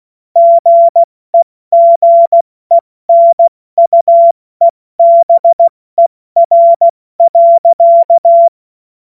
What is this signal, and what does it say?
Morse code, heard as a keyed tone, about 16 wpm, 680 Hz. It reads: GEGENUEBER.
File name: Morsekurs / Witze / Witz 29. Juni 2021, 70 F80